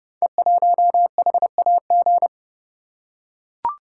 Morse code, heard as a keyed tone, about 28 wpm, 700 Hz. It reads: E1HAG
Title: Code Groups - 5 Letter:Numbers - 30wpm